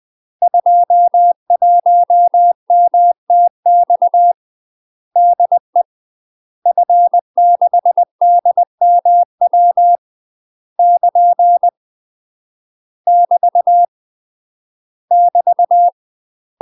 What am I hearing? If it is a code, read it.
Morse code, heard as a keyed tone, about 20 wpm, 700 Hz. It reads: 21MTX DE F6DMW <KN> <BT> <BT>
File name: UK MASTER QSOs TRAINING/20WPM